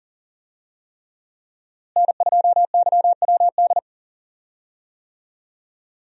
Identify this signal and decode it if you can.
Morse code, heard as a keyed tone, about 40 wpm, 700 Hz. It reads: N2YWD
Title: USA Call Signs - 5 Characters - Repeated 3x - 40wpm